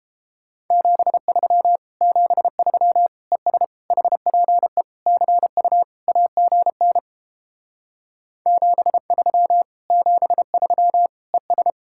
Morse code, heard as a keyed tone, about 31 wpm, 700 Hz. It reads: 73 73 ES HPE CU AGN 73 73 ES